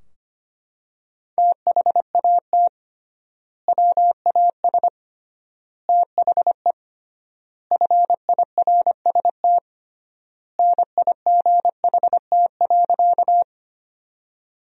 Morse code, heard as a keyed tone, about 25 wpm, 700 Hz. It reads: THAT WAS THE FIRST NIGHT.
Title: 3x Word-Spacing - Sentences from Top 300 Words - 25wpm